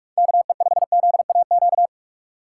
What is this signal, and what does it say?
Morse code, heard as a keyed tone, about 47 wpm, 700 Hz. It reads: KE5ZAQ
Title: USA Call Signs - 3x Character Spacing - 45wpm